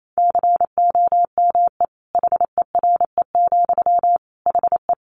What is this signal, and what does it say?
Morse code, heard as a keyed tone, about 28 wpm, 700 Hz. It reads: COME HERE, HE
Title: Sentences from Top 1500 Words 28wpm